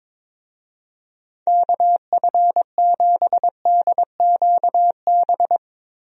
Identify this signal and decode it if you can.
Morse code, heard as a keyed tone, about 22 wpm, 700 Hz. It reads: KF7DQB
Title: USA Call Signs - 6 Characters - Repeated 2x - 22wpm